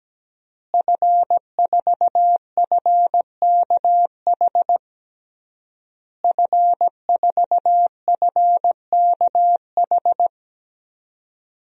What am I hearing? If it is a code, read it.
Morse code, heard as a keyed tone, about 17 wpm, 700 Hz. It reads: F4FKH F4FKH
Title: DXCC-100 Call Signs - 5 Characters - Repeated 3x - 17wpm